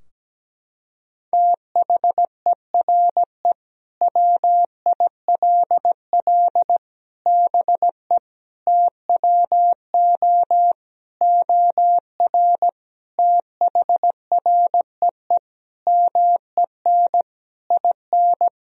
Morse code, heard as a keyed tone, about 17 wpm, 700 Hz. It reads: THERE WILL BE TWO OR THREE MEN IN
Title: Sentences from Top 1000 Words - Review Part 3 - 17wpm